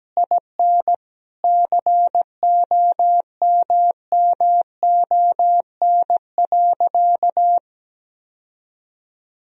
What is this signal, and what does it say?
Morse code, heard as a keyed tone, about 17 wpm, 700 Hz. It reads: IN COMMON.
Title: Sentences from Top 1000 Words - Review Part 3 - 17wpm